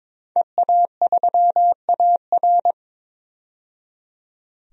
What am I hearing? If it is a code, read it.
Morse code, heard as a keyed tone, about 22 wpm, 700 Hz. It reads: EA3AR
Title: DXCC-100 Call Signs - 5 Characters - 22wpm